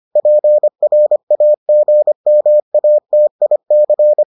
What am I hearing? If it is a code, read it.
Morse code, heard as a keyed tone, about 25 wpm, 600 Hz. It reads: PRAGMATIC